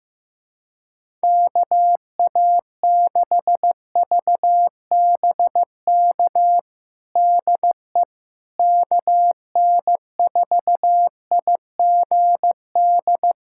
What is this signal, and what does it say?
Morse code, heard as a keyed tone, about 15 wpm, 700 Hz. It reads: KA6VBK DE KN4IGD
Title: USA Callsign DE Callsign 15wpm